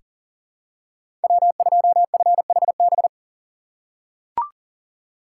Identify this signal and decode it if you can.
Morse code, heard as a keyed tone, about 40 wpm, 700 Hz. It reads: W2FHB
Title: USA Call Signs - 5 Characters - 6x Character Spacing - 40wpm